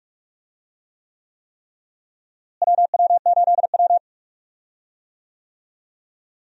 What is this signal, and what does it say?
Morse code, heard as a keyed tone, about 41 wpm, 700 Hz. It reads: WW8W
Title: USA Call Signs - 4 Characters - Repeated 3x - 45wpm